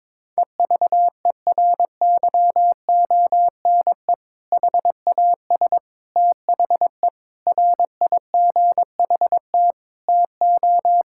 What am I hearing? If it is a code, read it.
Morse code, heard as a keyed tone, about 22 wpm, 700 Hz. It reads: EVERYONE HAS THE RIGHT TO
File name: practice / 22wpm / Sentences with 3-Word Phrases - 22wpm